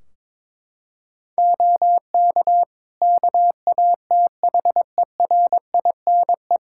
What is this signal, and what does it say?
Morse code, heard as a keyed tone, about 22 wpm, 700 Hz. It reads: OK KATHERINE
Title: Practice QSO - Exchange 3 - More Info - 22wpm